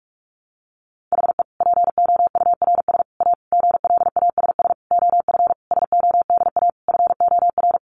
Morse code, heard as a keyed tone, about 44 wpm, 700 Hz. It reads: HE POURS A GLASS OF SODA FOR
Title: Additional Easy Phrases 45wpm